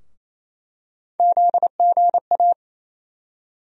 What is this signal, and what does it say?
Morse code, heard as a keyed tone, about 28 wpm, 700 Hz. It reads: ZGA
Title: USA Call Sign Postfixes - 3 Letters 28wpm